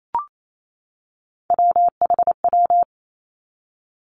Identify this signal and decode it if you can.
Morse code, heard as a keyed tone, about 28 wpm, 700 Hz. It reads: WHW
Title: Three Challenging Letter-Number Combinations 28wpm